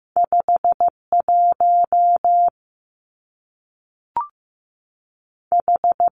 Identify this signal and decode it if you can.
Morse code, heard as a keyed tone, about 15 wpm, 700 Hz. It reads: 5 1 H
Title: Two Challenging Letter-Number Combinations 15wpm